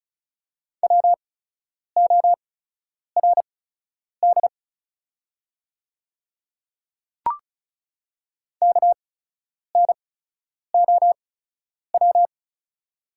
Morse code, heard as a keyed tone, about 36 wpm, spaced as for 10 wpm, 700 Hz. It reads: WORD KNOW
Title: 8x Character Spacing - Top 100 Words - 35wpm